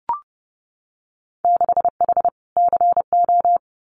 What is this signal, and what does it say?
Morse code, heard as a keyed tone, about 30 wpm, 700 Hz. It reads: 6H CO
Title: Winter Field Day Exchanges Encore 30wpm